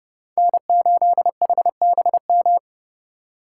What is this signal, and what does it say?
Morse code, heard as a keyed tone, about 34 wpm, 700 Hz. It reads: N8HBM
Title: Speed Racing - USA 1x3 Call Signs 30wpm